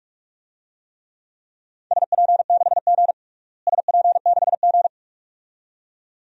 Morse code, heard as a keyed tone, about 46 wpm, 700 Hz. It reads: SP6G SP6G
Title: DXCC-100 Call Signs - 4 Characters - Repeated 2x - 45wpm